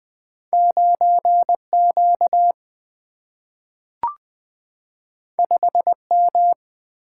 Morse code, heard as a keyed tone, about 20 wpm, 700 Hz. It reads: 9 Q 5 M
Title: Single-Letter Single-Number 20wpm